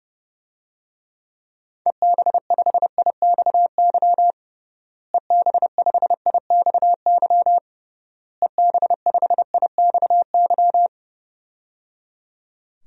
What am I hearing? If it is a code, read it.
Morse code, heard as a keyed tone, about 30 wpm, 700 Hz. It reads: EB5IXY EB5IXY EB5IXY
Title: DXCC-100 Call Signs - 6 Characters - Repeated 3x - 30wpm